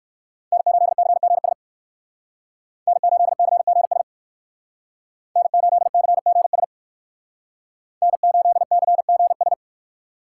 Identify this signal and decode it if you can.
Morse code, heard as a keyed tone, about 59 wpm, 700 Hz. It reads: N8CGS N8CGS N8CGS N8CGS